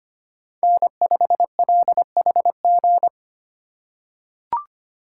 Morse code, heard as a keyed tone, about 25 wpm, 700 Hz. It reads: N5LHG
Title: USA Call Signs - 5 Characters - 4x Character Spacing - 25wpm